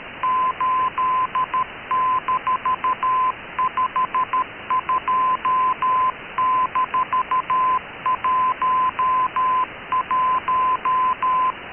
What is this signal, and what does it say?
Morse code, heard as a keyed tone, about 13 wpm, 1020 Hz. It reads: 8-52-11
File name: KKS-1 cw 21jul2014 1425z